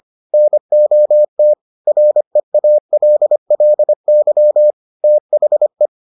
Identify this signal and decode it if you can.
Morse code, heard as a keyed tone, about 25 wpm, 600 Hz. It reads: NOT REALLY THE